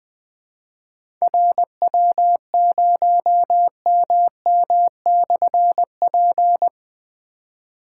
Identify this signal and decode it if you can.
Morse code, heard as a keyed tone, about 20 wpm, 700 Hz. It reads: RW0MM/P
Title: Difficult Call Signs 20wpm